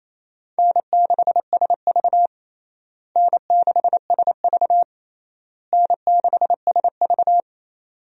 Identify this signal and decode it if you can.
Morse code, heard as a keyed tone, about 27 wpm, 700 Hz. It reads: N6SV N6SV N6SV
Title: USA Call Signs - 4 Characters - Repeated 3x - 28wpm